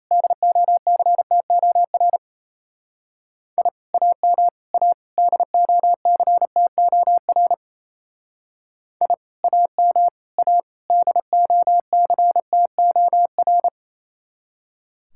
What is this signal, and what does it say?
Morse code, heard as a keyed tone, about 32 wpm, 700 Hz. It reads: DOCTOR I AM A DOCTOR I AM A DOCTOR
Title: Speed-Racing Common QSO Phrases - 28wpm